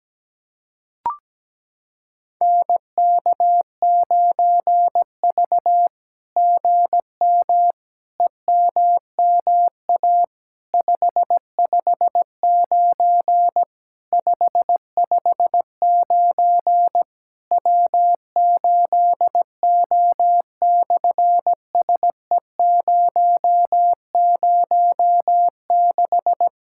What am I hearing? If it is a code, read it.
Morse code, heard as a keyed tone, about 17 wpm, 700 Hz. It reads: NK9V GM EMMA 559 559 W8O/SE006
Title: SOTA QSO Practice 17wpm